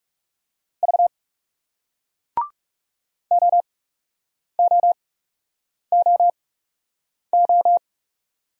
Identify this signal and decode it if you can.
Morse code, heard as a keyed tone, about 40 wpm, spaced as for 8 wpm, 700 Hz. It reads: V OOOO